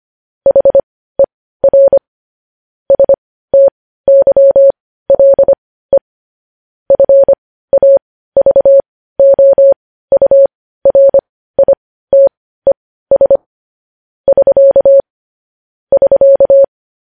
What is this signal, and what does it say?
Morse code, heard as a keyed tone, about 25 wpm, spaced as for 15 wpm, 550 Hz. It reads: HER STYLE FAVOURITES <SK> <SK>
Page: News Headlines in Morse Code at 15 WPM